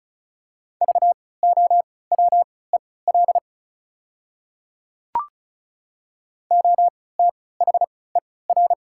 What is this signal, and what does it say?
Morse code, heard as a keyed tone, about 36 wpm, spaced as for 20 wpm, 690 Hz. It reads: VOWEL OTHER
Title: 3x Character Spacing - Top 500 Words - 35wpm